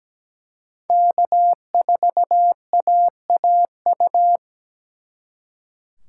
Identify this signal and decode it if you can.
Morse code, heard as a keyed tone, about 21 wpm, 700 Hz. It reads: K4AAU
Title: Speed Racing - USA 1x3 Call Signs 17wpm